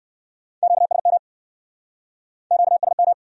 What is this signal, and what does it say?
Morse code, heard as a keyed tone, about 65 wpm, 700 Hz. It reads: QSM QSM